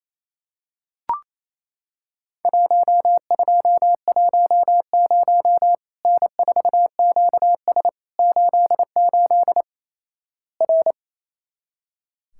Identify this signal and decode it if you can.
Morse code, heard as a keyed tone, about 28 wpm, 700 Hz. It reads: 1210 N4QS 88 R
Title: Worked All Europe DX Contest - QTC Exchange - 28wpm